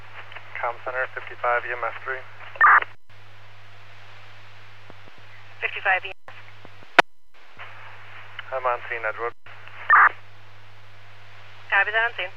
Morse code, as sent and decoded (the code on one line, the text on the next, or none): none